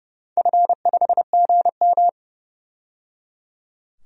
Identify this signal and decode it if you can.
Morse code, heard as a keyed tone, about 30 wpm, 700 Hz. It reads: F5GM
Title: DXCC-100 Call Signs - 4 Characters - 30wpm